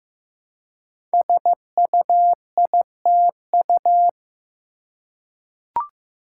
Morse code, heard as a keyed tone, about 15 wpm, 700 Hz. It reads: SUITU
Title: Code Groups - 5 Letters - 15wpm